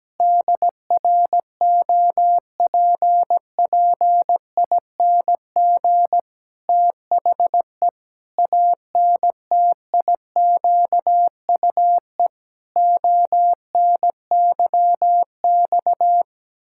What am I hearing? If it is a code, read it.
Morse code, heard as a keyed tone, about 17 wpm, 700 Hz. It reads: DROPPING THE ANTIQUE ONYX
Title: Pangram Challenge v2 - 17wpm